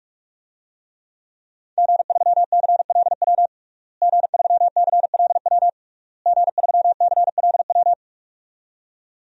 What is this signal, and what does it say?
Morse code, heard as a keyed tone, about 41 wpm, 700 Hz. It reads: G3CLW G3CLW G3CLW
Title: DXCC-100 Call Signs - 5 Characters - Repeated 3x - 45wpm